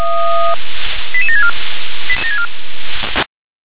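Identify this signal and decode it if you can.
Five-tone selective call (ZVEI), heard as quick runs of five tones, tones of about 70 ms. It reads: 88764
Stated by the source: ZVEI five-tone selective call 88775 received on SDR.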